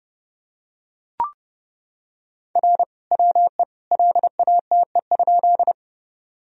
Morse code, heard as a keyed tone, about 31 wpm, 700 Hz. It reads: R WE LATE?
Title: Sentences from Top 300 Words and Abbreviations 30wpm